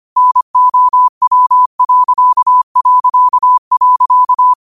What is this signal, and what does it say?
Morse code, heard as a keyed tone, about 25 wpm, 1000 Hz. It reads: NOW...
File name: qotdfiles / 25wpm